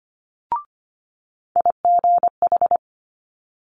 Morse code, heard as a keyed tone, about 24 wpm, 700 Hz. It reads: IGH
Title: Multi-letter Phonemes Quick Fire 25wpm